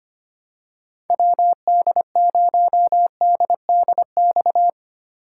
Morse code, heard as a keyed tone, about 25 wpm, 700 Hz. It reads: WD0DDX
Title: USA Call Signs - 6 Characters - 3x Character Spacing - 25wpm